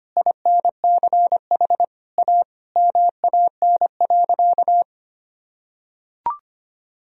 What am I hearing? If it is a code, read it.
Morse code, heard as a keyed tone, about 25 wpm, 700 Hz. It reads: INCH A MAN.
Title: Sentences from Top 1000 Words - Review Part 3 - 25wpm